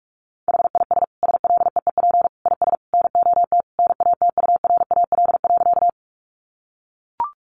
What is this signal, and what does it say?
Morse code, heard as a keyed tone, about 44 wpm, 700 Hz. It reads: HIS SLEEP IS NOT NATURAL.
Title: Sentences from Top 700 Words 45wpm